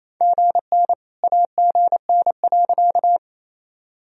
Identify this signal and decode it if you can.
Morse code, heard as a keyed tone, about 28 wpm, 700 Hz. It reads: GN AGN.
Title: Sentences from Top 300 Words and Abbreviations 28wpm